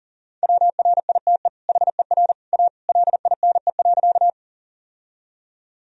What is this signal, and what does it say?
Morse code, heard as a keyed tone, about 40 wpm, 700 Hz. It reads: WRITE HER A LINE.
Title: Sentences from Top 1000 Words - Review Part 1 - 40wpm